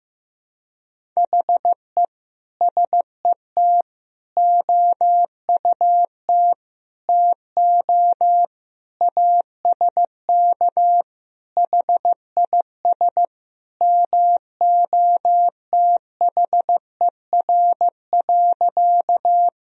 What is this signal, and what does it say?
Morse code, heard as a keyed tone, about 15 wpm, 700 Hz. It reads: HE SET OUT TO ASK HIS MOTHER.
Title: Sentences from Top 1000 Words - Review Part 1 - 15wpm